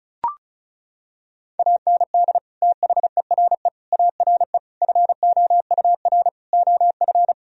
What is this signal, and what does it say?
Morse code, heard as a keyed tone, about 36 wpm, 700 Hz. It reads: AND THERE ARE FOUR OF
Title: Sentences from Top 1000 Words - Review Part 1 - 35wpm